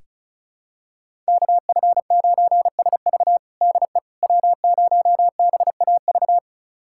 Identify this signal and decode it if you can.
Morse code, heard as a keyed tone, about 36 wpm, 700 Hz. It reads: KF9SV DE W0BAV
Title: USA Callsign DE Callsign - 5 Characters - 35wpm